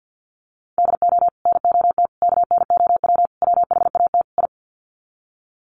Morse code, heard as a keyed tone, about 50 wpm, 700 Hz. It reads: DO NOT KNOW WHAT I
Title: Sets of 5 Words 50wpm